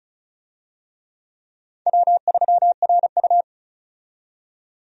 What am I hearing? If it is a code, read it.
Morse code, heard as a keyed tone, about 36 wpm, 700 Hz. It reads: W3RU
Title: USA Call Signs - 4 Characters - 35wpm